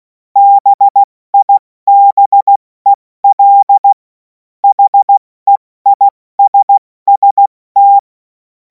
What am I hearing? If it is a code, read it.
Morse code, heard as a keyed tone, about 16 wpm, 800 Hz. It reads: BIBEL HEISST
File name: Morsekurs / Witze / Witz 24. Februar 2026 70 F80